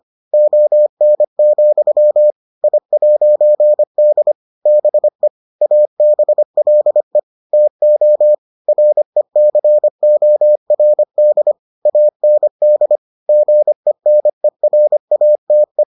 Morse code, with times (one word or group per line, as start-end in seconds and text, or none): 0.3-2.3 ON,
2.6-4.3 I'D
4.7-5.3 BE
5.6-7.2 ABLE
7.5-8.3 TO
8.7-11.5 RECORD
11.9-13.0 AND
13.3-15.8 GENERATE